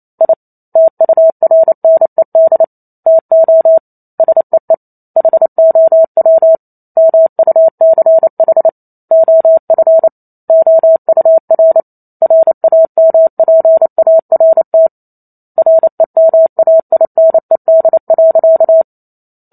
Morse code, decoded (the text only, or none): I TURNED TO SEE HOW MUCH OF OUR RAMPART REMAINED.